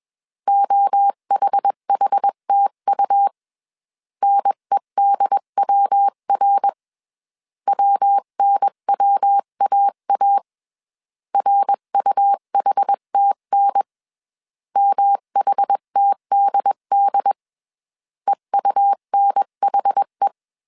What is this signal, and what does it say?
Morse code, heard as a keyed tone, about 21 wpm, 800 Hz. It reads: OHHTU NEDWR WNWAA RUHTN MHTDD EUNHE
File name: CW / DET / 1 3 1500C S B 20wpm 1754